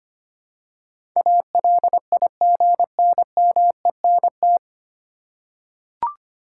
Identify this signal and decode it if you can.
Morse code, heard as a keyed tone, about 25 wpm, 700 Hz. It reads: ALIGNMENT
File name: Nine Letter Words - 25wpm